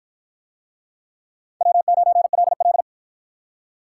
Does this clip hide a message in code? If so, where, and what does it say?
Yes: Morse code from 1.6 to 2.8 s, heard as a keyed tone, about 53 wpm, 700 Hz. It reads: W9LL